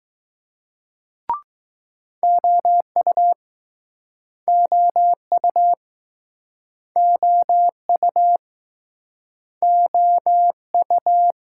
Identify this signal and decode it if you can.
Morse code, heard as a keyed tone, about 20 wpm, 700 Hz. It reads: OU OU OU OU